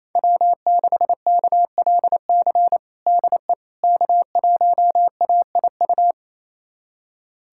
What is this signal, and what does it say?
Morse code, heard as a keyed tone, about 28 wpm, 700 Hz. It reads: W6KLC DE K1AIU
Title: USA Callsign DE Callsign - 5 Characters - 28wpm